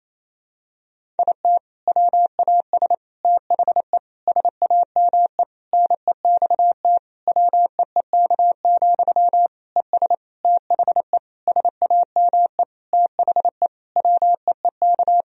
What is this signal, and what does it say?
Morse code, heard as a keyed tone, about 28 wpm, 700 Hz. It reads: IT WAS THE SAME NEXT WEEK, ES THE SAME THE WEEK
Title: Sentences from Top 500 Words and Abbreviations 28wpm